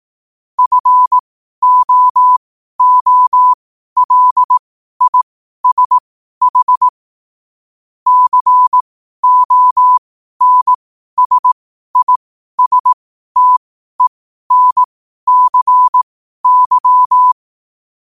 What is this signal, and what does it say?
Morse code, heard as a keyed tone, about 18 wpm, spaced as for 12 wpm, 1000 Hz. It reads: FOOLISH CONSISTENCY
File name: qotdfiles / 12wpm